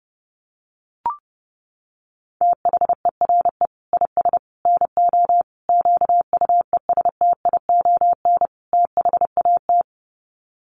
Morse code, heard as a keyed tone, about 30 wpm, 700 Hz. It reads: THERE IS NO QUESTION THAT
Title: Sets of 5 Words Encore 30wpm